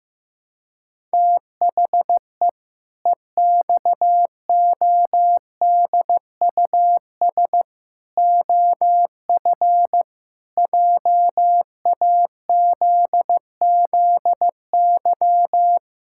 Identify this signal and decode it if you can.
Morse code, heard as a keyed tone, about 15 wpm, 700 Hz. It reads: THE EXODUS OF JAZZY
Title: Pangram Challenge v2 - 15wpm